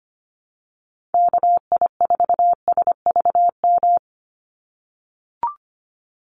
Morse code, heard as a keyed tone, about 25 wpm, 700 Hz. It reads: KI4SVM
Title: NA SOTA Chaser Callsigns 25wpm